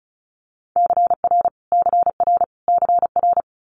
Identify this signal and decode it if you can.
Morse code, heard as a keyed tone, about 36 wpm, 700 Hz. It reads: CR CR CR
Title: Multi-letter Phonemes Repeated 3x 35wpm